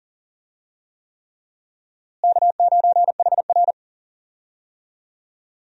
Morse code, heard as a keyed tone, about 40 wpm, 700 Hz. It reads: K9HR
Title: USA Call Signs - 4 Characters - Repeated 3x - 40wpm